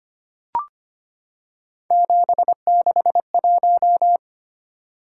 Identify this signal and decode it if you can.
Morse code, heard as a keyed tone, about 25 wpm, 700 Hz. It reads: 761
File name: Three Letter-Number Combinations - 25wpm